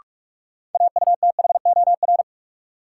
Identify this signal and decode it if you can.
Morse code, heard as a keyed tone, about 42 wpm, 700 Hz. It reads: AUTHOR
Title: Sets of 4 Words - New Word Drill - 45wpm